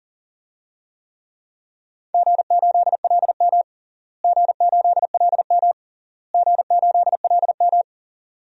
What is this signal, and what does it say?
Morse code, heard as a keyed tone, about 40 wpm, 700 Hz. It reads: G8LM G8LM G8LM
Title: DXCC-100 Call Signs - 4 Characters - Repeated 3x - 40wpm